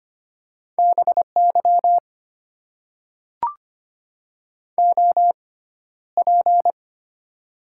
Morse code, heard as a keyed tone, about 25 wpm, 690 Hz. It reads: B Y O P
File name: Top 90 QSO Elements - 6x Character Spacing - 25wpm